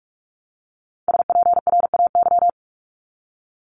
Morse code, heard as a keyed tone, about 46 wpm, 700 Hz. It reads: SPRAY
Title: Multi-letter Phonemes Repeated 3x 45wpm